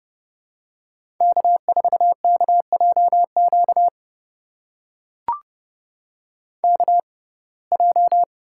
Morse code, heard as a keyed tone, about 30 wpm, spaced as for 22 wpm, 700 Hz. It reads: K4KJQ KJ
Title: USA Call Signs - 6x Character Spacing - 30wpm